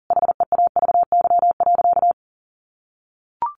Morse code, heard as a keyed tone, about 40 wpm, 700 Hz. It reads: HEAVY.
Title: Sentences from Top 1100 Words 40wpm